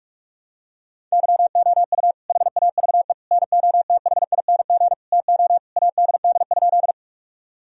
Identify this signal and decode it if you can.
Morse code, heard as a keyed tone, about 46 wpm, 700 Hz. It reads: YOU HAVE NOTHING TO ADD?
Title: Sentences from Top 1000 Words - Review Part 1 - 45wpm